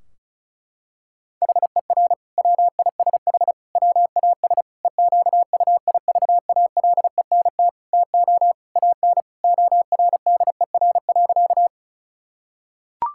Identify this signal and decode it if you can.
Morse code, heard as a keyed tone, about 34 wpm, 700 Hz. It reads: HER WISH WAS EQUIVALENT TO AN ORDER.
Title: Sentences from Top 1600 Words 35wpm